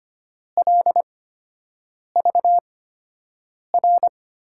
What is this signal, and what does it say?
Morse code, heard as a keyed tone, about 25 wpm, spaced as for 7 wpm, 700 Hz. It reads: LVR
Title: USA Call Signs - 6 Characters - 8x Character Spacing - 25wpm